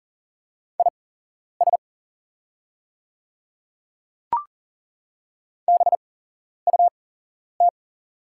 Morse code, heard as a keyed tone, about 40 wpm, spaced as for 11 wpm, 690 Hz. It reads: IS BUT
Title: 8x Character Spacing - Top 100 Words - 40wpm